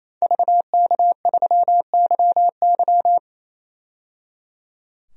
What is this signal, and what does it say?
Morse code, heard as a keyed tone, about 28 wpm, 700 Hz. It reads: VK3YY
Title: DXCC-100 Call Signs - 5 Characters - 28wpm